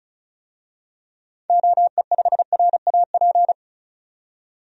Morse code, heard as a keyed tone, about 34 wpm, 700 Hz. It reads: OE5RAP
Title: DXCC-100 Call Signs - 6 Characters - 35wpm